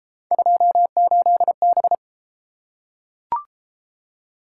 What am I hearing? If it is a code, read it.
Morse code, heard as a keyed tone, about 31 wpm, 700 Hz. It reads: 28B